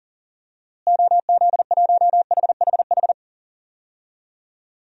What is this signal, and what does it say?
Morse code, heard as a keyed tone, about 40 wpm, 700 Hz. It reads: OZ1HHH
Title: DXCC-100 Call Signs - 6 Characters - Repeated 2x - 40wpm